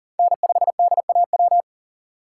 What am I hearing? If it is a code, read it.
Morse code, heard as a keyed tone, about 37 wpm, 700 Hz. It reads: N5DAW
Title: USA Call Signs - 5 Characters - 3x Character Spacing - 40wpm